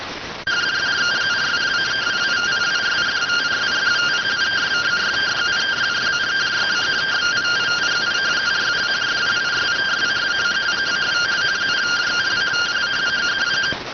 Two-tone teleprinter signal, about 170 Hz shift.